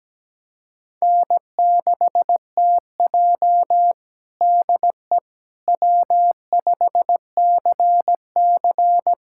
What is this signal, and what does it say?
Morse code, heard as a keyed tone, about 17 wpm, 700 Hz. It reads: N6TJ DE W5CC
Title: USA Callsign DE Callsign - 4 Characters - 17wpm